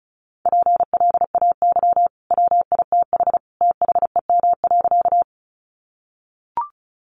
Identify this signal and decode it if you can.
Morse code, heard as a keyed tone, about 34 wpm, 700 Hz. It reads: PLAY WITH THEM.
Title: Sentences from Top 200 Words 35wpm